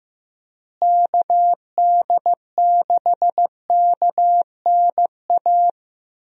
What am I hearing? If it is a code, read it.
Morse code, heard as a keyed tone, about 15 wpm, 700 Hz. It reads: KD6KNA
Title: USA Call Signs - 6 Characters - Repeated 2x - 15wpm